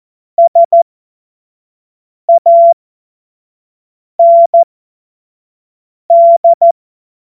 Morse code, heard as a keyed tone, about 14 wpm, spaced as for 5 wpm, 680 Hz. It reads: SAND
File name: Morsekurs / Witze / Witz 02. August 2022 25 F70